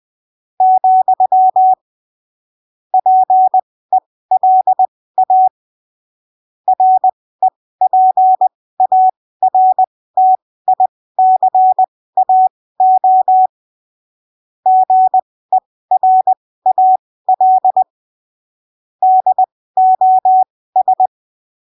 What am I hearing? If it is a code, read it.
Morse code, heard as a keyed tone, about 20 wpm, spaced as for 15 wpm, 750 Hz. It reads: , PELA REPARTICAO GERAL DOS
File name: arquivos / texto / texto 14 20ppm